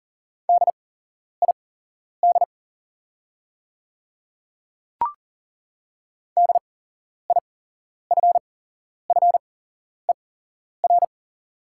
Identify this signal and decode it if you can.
Morse code, heard as a keyed tone, about 39 wpm, spaced as for 11 wpm, 700 Hz. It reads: DID DIFFER
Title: 8x Character Spacing - Top 200 Words - 40wpm